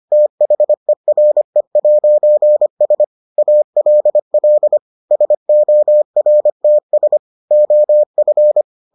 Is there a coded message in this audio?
Yes: Morse code from 0.1 to 8.6 s, heard as a keyed tone, about 25 wpm, 600 Hz. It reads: THERE'S ALL SORTS OF